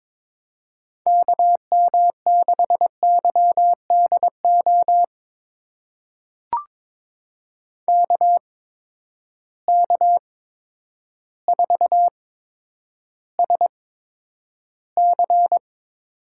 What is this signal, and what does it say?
Morse code, heard as a keyed tone, about 22 wpm, spaced as for 10 wpm, 700 Hz. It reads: KM6YDO KK4SC
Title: USA Call Signs - 6 Characters - 8x Character Spacing - 22wpm